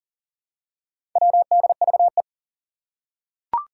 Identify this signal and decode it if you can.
Morse code, heard as a keyed tone, about 40 wpm, 700 Hz. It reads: WDVE
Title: Four Challenging Letter-Number Combinations 40wpm